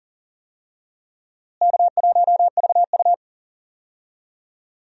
Morse code, heard as a keyed tone, about 40 wpm, 700 Hz. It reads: K1VU
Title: USA Call Signs - 4 Characters - Repeated 3x - 40wpm